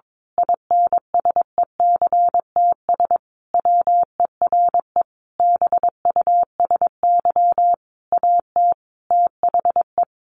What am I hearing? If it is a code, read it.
Morse code, heard as a keyed tone, about 22 wpm, 700 Hz. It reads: INSECTS WERE BUSY AT THE
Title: A Sound Of Thunder 22wpm